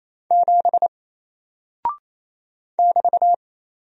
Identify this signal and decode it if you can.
Morse code, heard as a keyed tone, about 27 wpm, 700 Hz. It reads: 7 <BT>
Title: Single Character (Letter:Number:Punctuation:Prosign) 28wpm